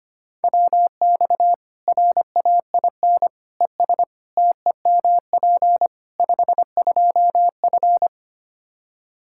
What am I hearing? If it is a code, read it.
Morse code, heard as a keyed tone, about 25 wpm, 700 Hz. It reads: WX RAIN ES TEMP 52F
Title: Practice QSO - Exchange 2 - Info - 25wpm